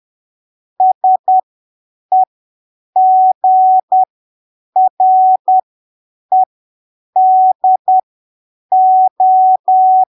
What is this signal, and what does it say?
Morse code, heard as a keyed tone, about 10 wpm, spaced as for 7 wpm, 750 Hz. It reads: SEGREDO
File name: arquivos / texto / texto 20 10ppm